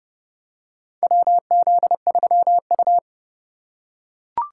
Morse code, heard as a keyed tone, about 30 wpm, 700 Hz. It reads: WZ3U
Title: USA Call Signs - 2x Character Spacing - 30wpm